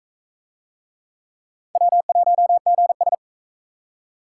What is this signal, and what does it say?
Morse code, heard as a keyed tone, about 37 wpm, 700 Hz. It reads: W1GS